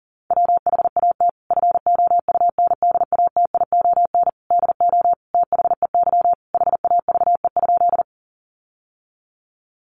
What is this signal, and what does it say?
Morse code, heard as a keyed tone, about 40 wpm, 700 Hz. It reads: WHAT FOUNDATION DO THEY HAVE?
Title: Sentences from Top 1200 Words 40wpm